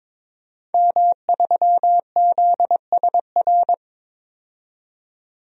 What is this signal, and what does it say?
Morse code, heard as a keyed tone, about 23 wpm, 700 Hz. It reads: M3ZSR
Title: DXCC-100 Call Signs - 5 Characters - 22wpm